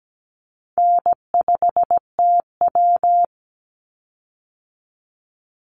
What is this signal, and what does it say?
Morse code, heard as a keyed tone, about 17 wpm, 700 Hz. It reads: N5TW
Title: Easy Call Signs - 17wpm